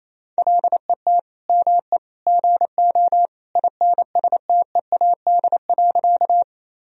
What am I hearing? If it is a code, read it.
Morse code, heard as a keyed tone, about 28 wpm, 700 Hz. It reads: LET ME GO INSTEAD.
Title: Sentences from Top 1000 Words - Review Part 4 - 28wpm